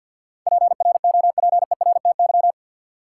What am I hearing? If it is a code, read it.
Morse code, heard as a keyed tone, about 48 wpm, 700 Hz. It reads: PROPERTY
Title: Sentences from Top 1000 Words - New Word Drill - 50wpm